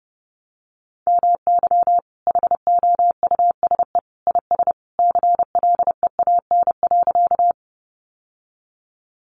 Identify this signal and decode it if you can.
Morse code, heard as a keyed tone, about 30 wpm, 700 Hz. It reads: MY HOUSE IS CLEAN.
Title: Sentences from Top 1100 Words 30wpm